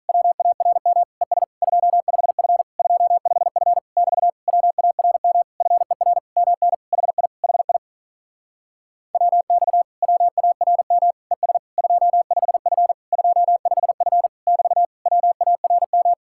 Morse code, heard as a keyed tone, about 46 wpm, 700 Hz. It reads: WARM ES 25F 25F <BT> WARM FER MN HI HI WX WARM ES 25F 25F <BT> WARM